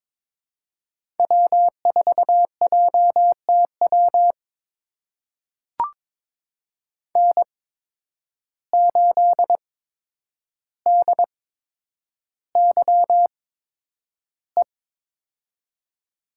Morse code, heard as a keyed tone, about 22 wpm, spaced as for 9 wpm, 700 Hz. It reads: W4JTW N8DYE
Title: USA Call Signs - 5 Characters - 8x Character Spacing - 22wpm